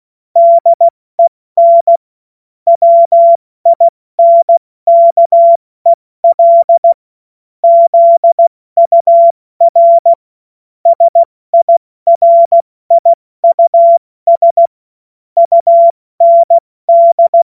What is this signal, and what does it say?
Morse code, heard as a keyed tone, about 16 wpm, 680 Hz. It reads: DEN WINKEL ZUR SIRIUS UND